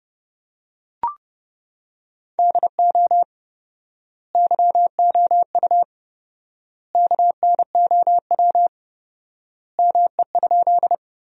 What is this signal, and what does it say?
Morse code, heard as a keyed tone, about 30 wpm, 700 Hz. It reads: DO YOU KNOW ME?
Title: 4x Word-Spacing - Sentences from Top 200 Words - 30wpm